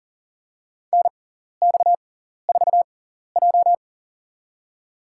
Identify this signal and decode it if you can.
Morse code, heard as a keyed tone, about 40 wpm, spaced as for 14 wpm, 700 Hz. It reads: NX4J
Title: USA Call Signs - 4 Characters - 6x Character Spacing - 40wpm